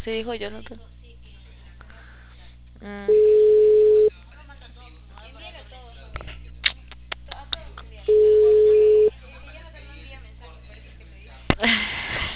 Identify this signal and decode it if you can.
Ringback tone (Europe, 425 Hz), with speech.